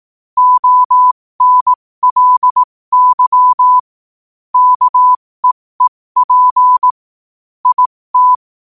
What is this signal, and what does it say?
Morse code, heard as a keyed tone, about 18 wpm, spaced as for 15 wpm, 1000 Hz. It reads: ONLY KEEP IT
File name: qotdfiles / 15wpm